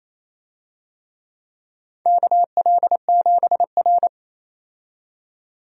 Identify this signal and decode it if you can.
Morse code, heard as a keyed tone, about 28 wpm, 700 Hz. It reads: KL7R
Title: USA Call Signs - 4 Characters - 28wpm